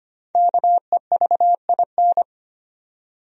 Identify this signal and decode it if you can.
Morse code, heard as a keyed tone, about 25 wpm, 700 Hz. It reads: KEVIN
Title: USA Common Names - Max 5 Letters - 25wpm